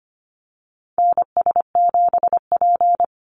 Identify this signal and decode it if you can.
Morse code, heard as a keyed tone, about 25 wpm, 690 Hz. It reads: NS7P
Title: NA SOTA Activator Callsigns 25wpm